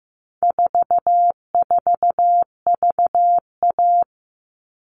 Morse code, heard as a keyed tone, about 15 wpm, 700 Hz. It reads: 44VA
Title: All Cut Numbers - 4-Digit 15wpm